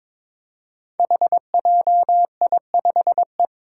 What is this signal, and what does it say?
Morse code, heard as a keyed tone, about 22 wpm, 700 Hz. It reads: HJI5E
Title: Five Challenging Letter-Number Combinations 22wpm